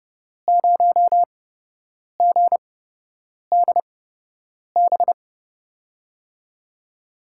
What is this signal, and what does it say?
Morse code, heard as a keyed tone, about 29 wpm, spaced as for 8 wpm, 700 Hz. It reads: 0GDB
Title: USA Call Signs - 6 Characters - 8x Character Spacing - 30wpm